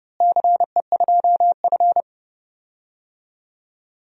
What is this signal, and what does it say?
Morse code, heard as a keyed tone, about 30 wpm, 700 Hz. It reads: CE2F
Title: DXCC-100 Call Signs - 4 Characters - Repeated 3x - 30wpm